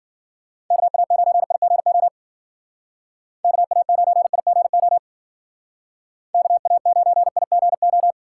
Morse code, heard as a keyed tone, about 52 wpm, 700 Hz. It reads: KA9IGO KA9IGO KA9IGO